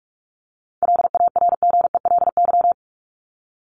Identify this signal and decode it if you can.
Morse code, heard as a keyed tone, about 44 wpm, 700 Hz. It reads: LARGELY
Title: LY Suffix - Top 100 Words with Base Words 45wpm